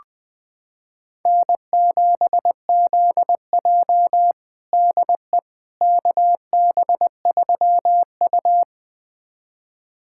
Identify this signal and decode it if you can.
Morse code, heard as a keyed tone, about 20 wpm, 700 Hz. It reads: N7ZJ DE KB3U
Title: USA Callsign DE Callsign - 4 Characters - 20wpm